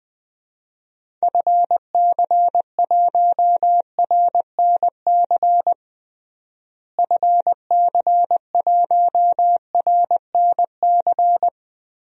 Morse code, heard as a keyed tone, about 21 wpm, 700 Hz. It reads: FC1RNC FC1RNC
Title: DXCC-100 Call Signs - 6 Characters - Repeated 3x - 20wpm